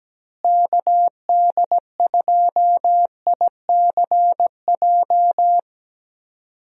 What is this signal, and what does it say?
Morse code, heard as a keyed tone, about 17 wpm, 700 Hz. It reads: KD2ICJ
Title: USA Call Signs - 6 Characters - Repeated 3x - 17wpm